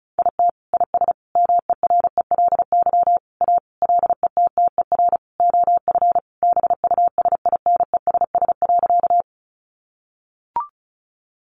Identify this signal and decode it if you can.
Morse code, heard as a keyed tone, about 34 wpm, 700 Hz. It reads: IT IS MERELY A LETTER OF BUSINESS.
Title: Sentences from Top 1400 Words 35wpm